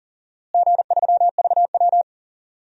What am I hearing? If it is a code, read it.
Morse code, heard as a keyed tone, about 37 wpm, 700 Hz. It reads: G3VW
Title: DXCC-100 Call Signs - 4 Characters - Repeated 2x - 40wpm